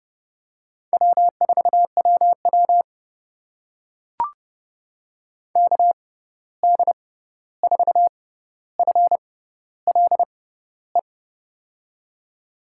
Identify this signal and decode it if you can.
Morse code, heard as a keyed tone, about 32 wpm, spaced as for 14 wpm, 700 Hz. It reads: W4WW KD4FLE
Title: USA Call Signs - 6x Character Spacing - 30wpm